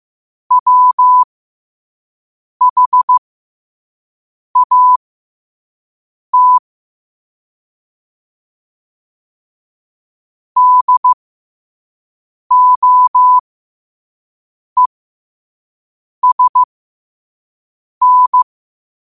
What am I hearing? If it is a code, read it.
Morse code, heard as a keyed tone, about 15 wpm, spaced as for 5 wpm, 1000 Hz. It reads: WHAT DOESN